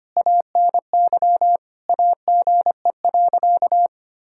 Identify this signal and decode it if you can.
Morse code, heard as a keyed tone, about 25 wpm, 700 Hz. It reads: ANY AGE.
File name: practice / 25wpm / Difficult Sentences with Hints - 25wpm